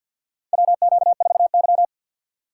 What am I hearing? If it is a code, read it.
Morse code, heard as a keyed tone, about 52 wpm, 700 Hz. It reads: WQ4Y
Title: USA Call Signs - 4 Characters - 3x Character Spacing - 50wpm